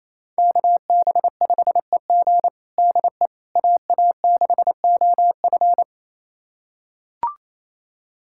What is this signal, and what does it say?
Morse code, heard as a keyed tone, about 28 wpm, 700 Hz. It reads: KB5EG DE AA6OF
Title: USA Callsign DE Callsign - 5 Characters - 28wpm